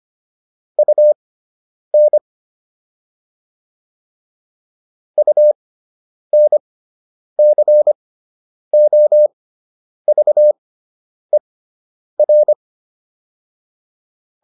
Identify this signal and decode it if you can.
Morse code, heard as a keyed tone, about 25 wpm, spaced as for 9 wpm, 600 Hz. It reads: UN UNCOVER